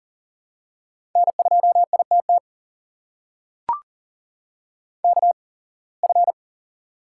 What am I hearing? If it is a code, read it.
Morse code, heard as a keyed tone, about 38 wpm, spaced as for 26 wpm, 700 Hz. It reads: N2ITT KF
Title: USA Call Signs - 5 Characters - 8x Character Spacing - 40wpm